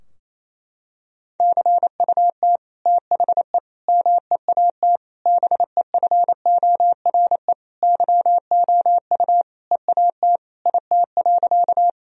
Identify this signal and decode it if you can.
Morse code, heard as a keyed tone, about 27 wpm, 700 Hz. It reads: CUT THE MEAT BEFORE YOU EAT IT.
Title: Sentences from Top 1000 Words - Review Part 3 - 28wpm